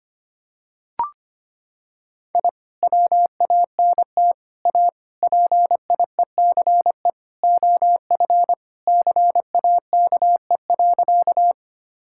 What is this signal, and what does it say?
Morse code, heard as a keyed tone, about 25 wpm, 700 Hz. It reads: I WANT A PIECE OF CAKE.
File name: practice / 25wpm / Sentences with 3-Word Phrases - 25wpm